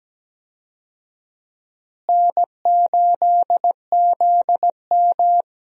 Morse code, heard as a keyed tone, about 17 wpm, 700 Hz. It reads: N8ZM
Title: USA Call Signs - 4 Characters - Repeated 3x - 17wpm